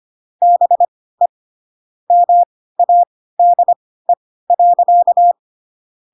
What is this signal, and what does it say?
Morse code, heard as a keyed tone, about 25 wpm, spaced as for 16 wpm, 700 Hz. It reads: BE MADE.
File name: Pinocchio / Pinocchio ch3 10 25 ik1hge